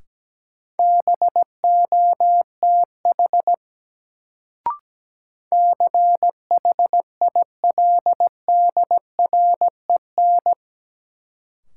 Morse code, heard as a keyed tone, about 17 wpm, 700 Hz. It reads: BOTH CHILDREN
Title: Sentences from Top 300 Words - New Word Drill - 17wpm